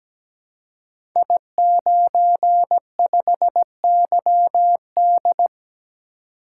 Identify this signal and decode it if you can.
Morse code, heard as a keyed tone, about 17 wpm, 700 Hz. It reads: I95YD
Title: Code Groups - 5 Letter:Numbers - 17wpm